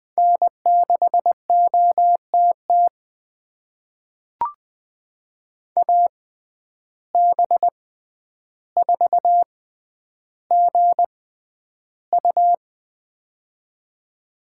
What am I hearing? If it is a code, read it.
Morse code, heard as a keyed tone, about 20 wpm, spaced as for 10 wpm, 700 Hz. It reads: N6OTT AB4GU
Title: USA Call Signs - 5 Characters - 6x Character Spacing - 20wpm